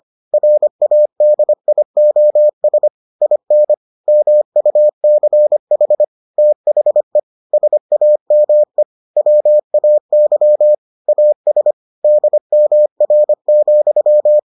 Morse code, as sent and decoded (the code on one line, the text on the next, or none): .-. .- -.. .. --- ... / .. -. / -- ..- -.-. .... / - .... / ... .- -- . / .-- .- -.-- / .- ... / -.. -- .-. --..--
RADIOS IN MUCH THE SAME WAY AS DMR,